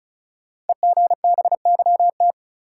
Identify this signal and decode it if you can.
Morse code, heard as a keyed tone, about 36 wpm, 700 Hz. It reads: EGBYT
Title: Code Groups - 5 Letter:Numbers - 35wpm